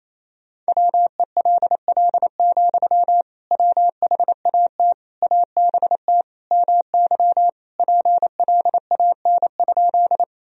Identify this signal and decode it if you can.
Morse code, heard as a keyed tone, about 28 wpm, 700 Hz. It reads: WELL, WHAT ABT MY PLAN?
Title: Sentences from Top 500 Words and Abbreviations 28wpm